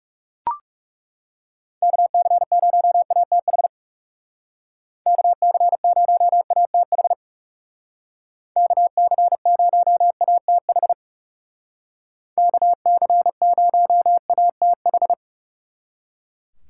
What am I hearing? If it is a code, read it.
Morse code, heard as a keyed tone, about 37 wpm, 700 Hz. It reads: KC0ATH KC0ATH KC0ATH KC0ATH
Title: Speed Racing - USA 2x3 Call Signs 30wpm